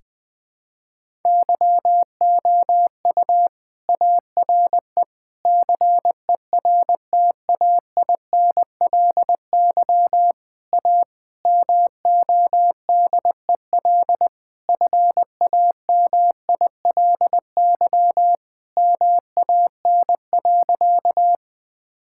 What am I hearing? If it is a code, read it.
Morse code, heard as a keyed tone, about 20 wpm, 700 Hz. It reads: YOU ARE CERTAINLY A MODEL FAMILY MAN.
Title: Sentences from Top 1000 Words - Review Part 5 - 20wpm